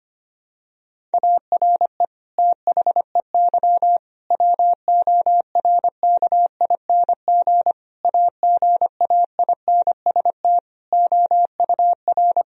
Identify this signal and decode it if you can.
Morse code, heard as a keyed tone, about 25 wpm, 700 Hz. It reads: ARE THEY WORKING AGAINST OUR
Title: Sentences from Top 1000 Words - Review Part 4 - 25wpm